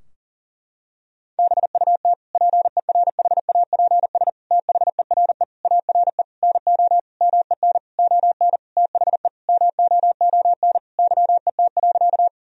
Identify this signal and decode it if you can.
Morse code, heard as a keyed tone, about 40 wpm, 700 Hz. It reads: BUT PERHAPS THERE ARE NO MEN ON THE MOON YET.
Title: Sentences from Top 1000 Words - Review Part 3 - 40wpm